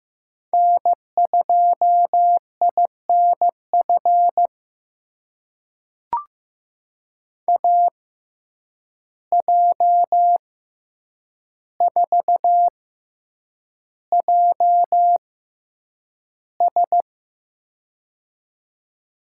Morse code, heard as a keyed tone, about 15 wpm, spaced as for 8 wpm, 700 Hz. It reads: N2INF AJ4JS
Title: USA Call Signs - 5 Characters - 6x Character Spacing - 15wpm